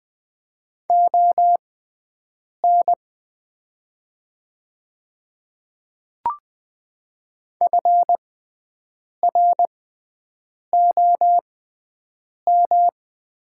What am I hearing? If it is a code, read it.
Morse code, heard as a keyed tone, about 20 wpm, spaced as for 7 wpm, 700 Hz. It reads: ON FROM
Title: 6x Character Spacing - Top 200 Words - 20wpm